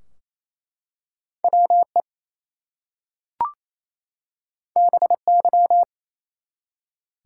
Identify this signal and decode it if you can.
Morse code, heard as a keyed tone, about 29 wpm, 700 Hz. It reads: WE BY
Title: Two Letter Words - 28wpm